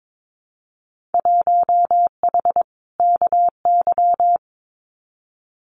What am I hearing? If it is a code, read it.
Morse code, heard as a keyed tone, about 22 wpm, 700 Hz. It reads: 1H KY
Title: Winter Field Day Exchanges 22wpm